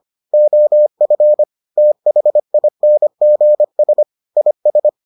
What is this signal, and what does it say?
Morse code, heard as a keyed tone, about 25 wpm, 600 Hz. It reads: OF THINGS IS